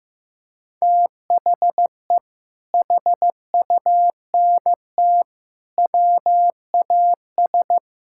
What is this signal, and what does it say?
Morse code, heard as a keyed tone, about 15 wpm, 700 Hz. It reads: THE HUNT WAS
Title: Sentences from Top 1600 Words 15wpm